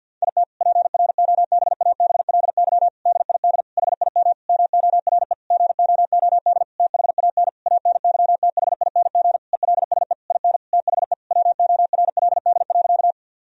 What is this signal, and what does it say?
Morse code, heard as a keyed tone, about 50 wpm, 700 Hz. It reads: IT PROBABLY DID HIM MORE GOOD THAN ANYTHING ELSE IN THE WORLD.
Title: Sentences from Top 1000 Words - Review Part 4 - 50wpm